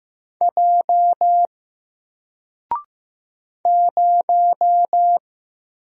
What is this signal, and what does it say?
Morse code, heard as a keyed tone, about 15 wpm, 700 Hz. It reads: J 0
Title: Single Letter-Number Alt 15wpm